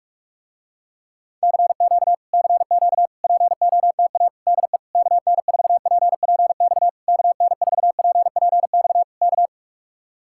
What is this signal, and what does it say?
Morse code, heard as a keyed tone, about 45 wpm, 700 Hz. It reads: CQ CQ POTA DE KN4PPX KN4PPX K